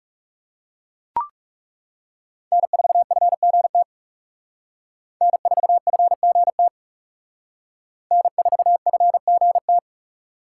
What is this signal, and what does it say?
Morse code, heard as a keyed tone, about 37 wpm, 700 Hz. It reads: N4FGT N4FGT N4FGT
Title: Speed Racing - USA 1x3 Call Signs 30wpm